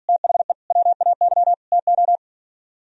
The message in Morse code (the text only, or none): THE WAY TO